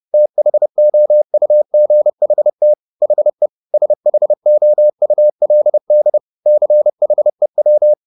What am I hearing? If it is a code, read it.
Morse code, heard as a keyed tone, about 30 wpm, 600 Hz. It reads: THOUGHT HE SHOULD CHEW